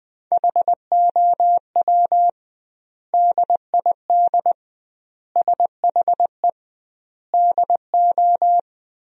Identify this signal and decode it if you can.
Morse code, heard as a keyed tone, about 20 wpm, 700 Hz. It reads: HOW DID SHE DO
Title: 2x Word-Spacing - Sentences from Top 100 Words - 20wpm